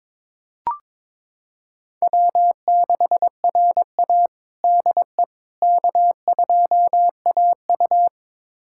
Morse code, heard as a keyed tone, about 22 wpm, 700 Hz. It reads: W6RA DE K2AU
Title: USA Callsign DE Callsign - 4 Characters - 22wpm